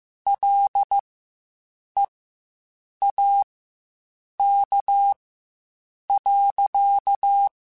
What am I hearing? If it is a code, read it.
Morse code, heard as a keyed tone, about 15 wpm, spaced as for 7 wpm, 800 Hz. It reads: LEAK.
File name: Quotes Spaced / Quotes Spaced-mp3s / Airtight sp @10